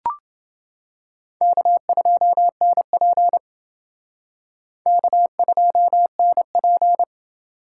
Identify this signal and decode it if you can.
Morse code, heard as a keyed tone, about 28 wpm, 700 Hz. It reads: K2NP K2NP